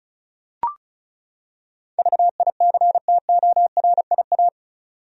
Morse code, heard as a keyed tone, about 35 wpm, 700 Hz. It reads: VICTORIA
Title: USA Common Names - 35wpm